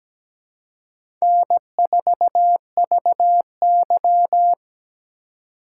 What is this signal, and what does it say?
Morse code, heard as a keyed tone, about 17 wpm, 700 Hz. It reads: N4VY
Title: USA Call Signs - 4 Characters - 4x Character Spacing - 17wpm